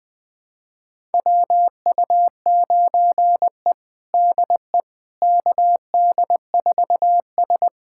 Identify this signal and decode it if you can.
Morse code, heard as a keyed tone, about 21 wpm, 700 Hz. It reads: WU9E DE KD4S
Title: USA Callsign DE Callsign - 4 Characters - 20wpm